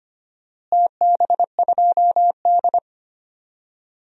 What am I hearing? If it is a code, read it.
Morse code, heard as a keyed tone, about 25 wpm, 700 Hz. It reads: TB2D
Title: DXCC-100 Call Signs - 4 Characters - Repeated 2x - 25wpm